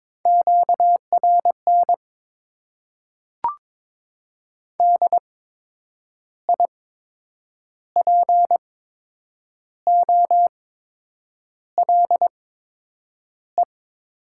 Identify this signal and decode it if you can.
Morse code, heard as a keyed tone, about 22 wpm, spaced as for 8 wpm, 700 Hz. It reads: QRN DIPOLE
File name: Top 90 QSO Elements - 8x Character Spacing - 22wpm